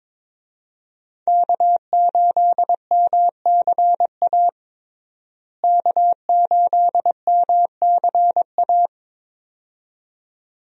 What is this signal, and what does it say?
Morse code, heard as a keyed tone, about 22 wpm, 700 Hz. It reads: K8MCA K8MCA
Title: USA Call Signs - 5 Characters - Repeated 2x - 22wpm